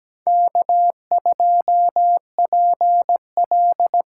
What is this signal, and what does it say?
Morse code, heard as a keyed tone, about 17 wpm, 700 Hz. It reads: K2PL
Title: USA Call Signs - 5 Characters - 8x Character Spacing - 17wpm